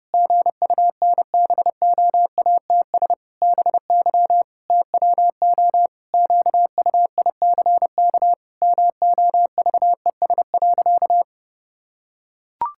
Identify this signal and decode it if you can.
Morse code, heard as a keyed tone, about 31 wpm, 700 Hz. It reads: GUNBOATS BY TWO QUICK MOVES.
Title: Pangram Challenge v2 - 30wpm